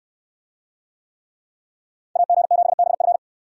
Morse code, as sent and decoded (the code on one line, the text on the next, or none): .- --. --... -... ..-.
AG7BF